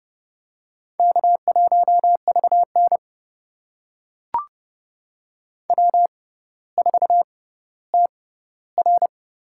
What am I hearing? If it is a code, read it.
Morse code, heard as a keyed tone, about 31 wpm, spaced as for 16 wpm, 700 Hz. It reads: K1VN W4TR
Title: USA Call Signs - 4 Characters - 6x Character Spacing - 30wpm